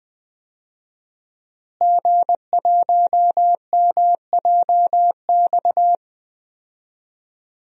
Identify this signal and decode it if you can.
Morse code, heard as a keyed tone, about 20 wpm, 700 Hz. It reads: G1MJX
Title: DXCC-100 Call Signs - 5 Characters - Repeated 3x - 20wpm